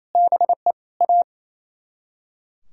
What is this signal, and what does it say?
Morse code, heard as a keyed tone, about 34 wpm, 700 Hz. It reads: BE A
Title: Speed-Racing - Sets of 3 Words 28wpm